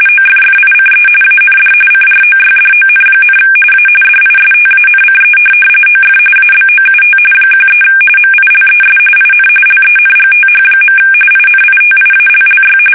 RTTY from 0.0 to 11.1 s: OFFICE AND CROYDON, A DISTANCE OF ABOUT 13 MILES IN LENGTH.